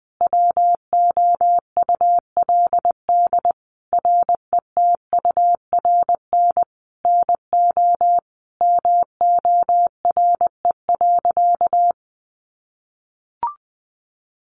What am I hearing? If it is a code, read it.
Morse code, heard as a keyed tone, about 20 wpm, 700 Hz. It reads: WOULD RETURN NO MORE.
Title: Sentences from Top 800 Words 20wpm